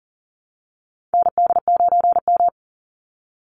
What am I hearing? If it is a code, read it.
Morse code, heard as a keyed tone, about 40 wpm, 700 Hz. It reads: ND9M
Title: Call Signs - 40wpm